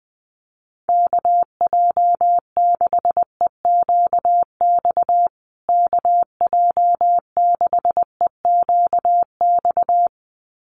Morse code, heard as a keyed tone, about 20 wpm, 700 Hz. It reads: KJ6EQX KJ6EQX
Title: USA Call Signs Repeated 2x 20wpm